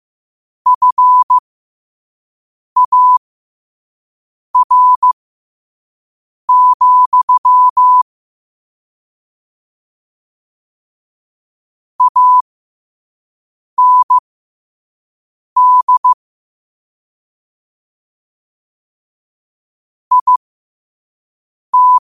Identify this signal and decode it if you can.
Morse code, heard as a keyed tone, about 15 wpm, spaced as for 5 wpm, 1000 Hz. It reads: FAR, AND IT
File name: qotdfiles / 5wpm